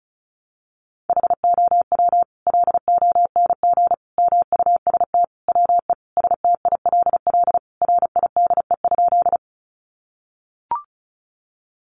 Morse code, heard as a keyed tone, about 34 wpm, 700 Hz. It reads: HOW LONG MUST WE STILL RIDE?
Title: Sentences from Top 1200 Words 35wpm